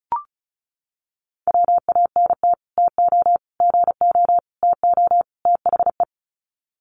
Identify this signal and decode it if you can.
Morse code, heard as a keyed tone, about 34 wpm, 700 Hz. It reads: WANT TO GO TO THE
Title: Sets of 5 Words Encore2 35wpm